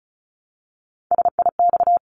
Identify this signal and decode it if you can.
Morse code, heard as a keyed tone, about 34 wpm, 700 Hz. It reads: SIX
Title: Top 500 Words Repeated 35wpm